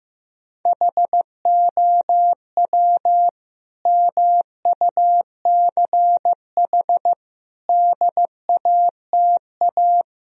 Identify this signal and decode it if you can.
Morse code, heard as a keyed tone, about 15 wpm, 700 Hz. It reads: HOW MUCH DATA
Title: Sentences from Top 1000 Words - Review Part 4 - 15wpm